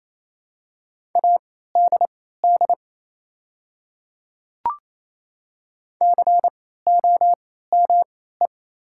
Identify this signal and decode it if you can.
Morse code, heard as a keyed tone, about 28 wpm, spaced as for 16 wpm, 700 Hz. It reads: ADD COME
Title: 3x Character Spacing - Top 200 Words - 28wpm